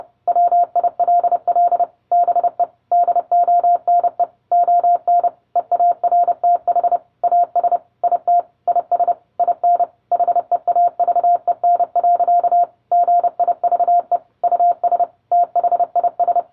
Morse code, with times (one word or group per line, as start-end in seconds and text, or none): 0.3-1.8 WILL
2.1-2.6 BE
2.9-4.2 DONE
4.5-5.3 ON
5.6-7.0 EARTH
7.2-7.8 AS
8.0-8.4 IT
8.7-9.1 IS
9.4-9.8 IN
10.1-12.6 HEAVEN.
12.9-14.2 GIVE
14.4-15.0 US
15.3-16.4 THIS